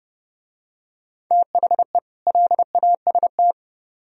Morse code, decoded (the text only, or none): THE LAST